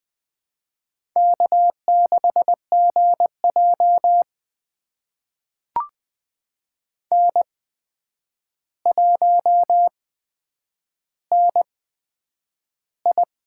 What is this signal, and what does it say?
Morse code, heard as a keyed tone, about 20 wpm, spaced as for 9 wpm, 700 Hz. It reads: K6GJ N1NI
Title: USA Call Signs - 4 Characters - 8x Character Spacing - 20wpm